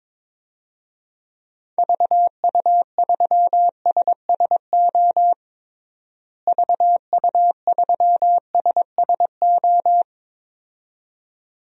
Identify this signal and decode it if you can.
Morse code, heard as a keyed tone, about 22 wpm, 700 Hz. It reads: VU3SSO VU3SSO
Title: DXCC-100 Call Signs - 6 Characters - Repeated 2x - 22wpm